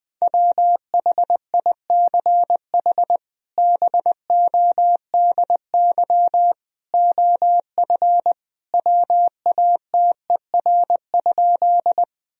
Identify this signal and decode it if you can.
Morse code, heard as a keyed tone, about 21 wpm, 700 Hz. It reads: WHICH BODY OF WATER?
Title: Sentences from Top 1000 Words - Review Part 2 - 20wpm